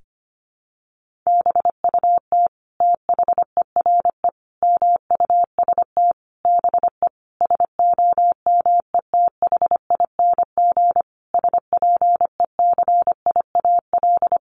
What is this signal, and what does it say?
Morse code, heard as a keyed tone, about 25 wpm, 700 Hz. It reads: BUT THERE MUST BE SOMETHING SPECIAL
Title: Sentences from Top 1500 Words 25wpm